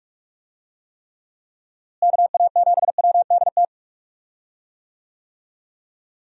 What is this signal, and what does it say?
Morse code, heard as a keyed tone, about 46 wpm, 700 Hz. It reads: KA7WDT
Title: USA Call Signs - 6 Characters - Repeated 3x - 45wpm